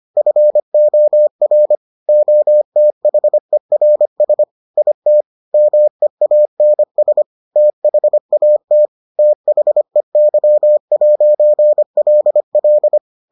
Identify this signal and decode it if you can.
Morse code, heard as a keyed tone, about 25 wpm, 600 Hz. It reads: FOR OTHERS IT MEANS THAT THEY'LL